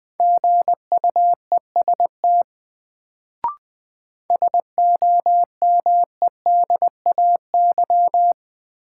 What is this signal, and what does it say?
Morse code, heard as a keyed tone, about 20 wpm, 700 Hz. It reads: GUEST SOMEDAY
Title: Sentences from Top 1600 Words - New Word Drill - 20wpm